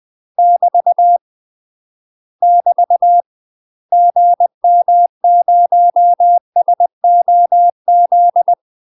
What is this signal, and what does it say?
Morse code, heard as a keyed tone, about 20 wpm, 700 Hz. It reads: <BT> <BT> GM0SOZ
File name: UK MASTER QSOs TRAINING / 20WPM / UK MASTER QSOs 0011